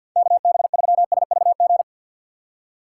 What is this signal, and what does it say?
Morse code, heard as a keyed tone, about 63 wpm, 700 Hz. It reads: KB3SVG